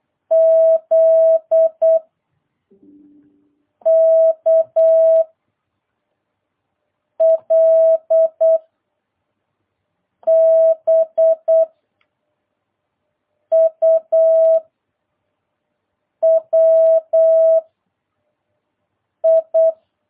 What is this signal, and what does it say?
Morse code, heard as a keyed tone, about 8 wpm, 650 Hz. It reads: Z K L B U W I